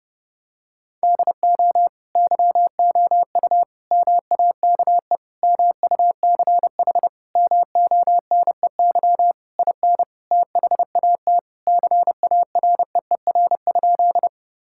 Morse code, heard as a keyed tone, about 30 wpm, 700 Hz. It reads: DO YOU MAKE MUCH MONEY IN THAT CAREER?
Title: Sentences from Top 1000 Words - Review Part 6 - 30wpm